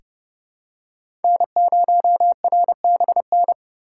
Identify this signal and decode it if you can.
Morse code, heard as a keyed tone, about 30 wpm, 700 Hz. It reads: N0RBN
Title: USA Call Signs - 5 Characters - 3x Character Spacing - 30wpm